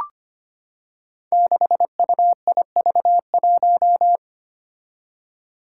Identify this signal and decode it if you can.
Morse code, heard as a keyed tone, about 25 wpm, 700 Hz. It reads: 6UIV1
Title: Code Groups - 5 Letter:Numbers - 25wpm